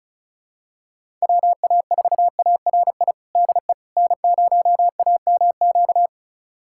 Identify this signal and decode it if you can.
Morse code, heard as a keyed tone, about 35 wpm, 700 Hz. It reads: WA4ARI DE N0AMQ
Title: USA Callsign DE Callsign 35wpm